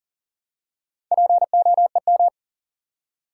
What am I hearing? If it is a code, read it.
Morse code, heard as a keyed tone, about 40 wpm, 700 Hz. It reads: POEM
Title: Four Letter Words - 40wpm